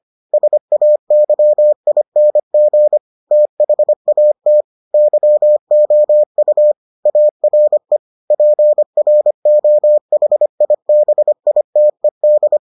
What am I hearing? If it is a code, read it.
Morse code, heard as a keyed tone, about 25 wpm, 600 Hz. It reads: SAYING THAT YOU ARE PROHIBITED